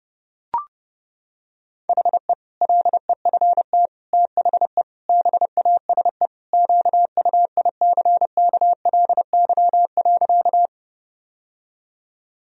Morse code, heard as a keyed tone, about 30 wpm, 700 Hz. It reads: HE LEFT THE BASE QUICKLY.
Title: Sentences from Top 1000 Words - Review Part 1 - 30wpm